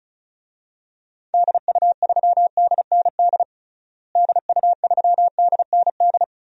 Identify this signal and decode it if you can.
Morse code, heard as a keyed tone, about 34 wpm, 700 Hz. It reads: DU3DND DU3DND
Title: DXCC-100 Call Signs - 6 Characters - Repeated 2x - 35wpm